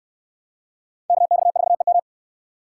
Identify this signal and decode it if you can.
Morse code, heard as a keyed tone, about 64 wpm, 700 Hz. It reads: KX4EM